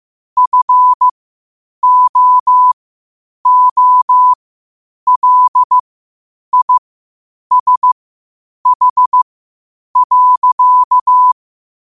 Morse code, heard as a keyed tone, about 15 wpm, spaced as for 8 wpm, 1000 Hz. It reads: FOOLISH.
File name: qotdfiles / 8wpm